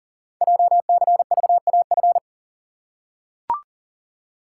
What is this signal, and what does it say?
Morse code, heard as a keyed tone, about 40 wpm, 700 Hz. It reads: JCVAF
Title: Code Groups - 5 Letters - 40wpm